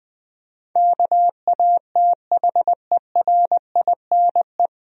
Morse code, heard as a keyed tone, about 20 wpm, 700 Hz. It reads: KATHERINE
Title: USA Common Names - 20wpm